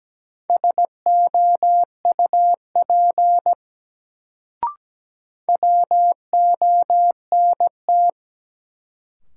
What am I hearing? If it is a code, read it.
Morse code, heard as a keyed tone, about 17 wpm, 700 Hz. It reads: SOUP WONT
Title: Sentences from Top 1000 Words - Review Part 6 - New Word Drill - 17wpm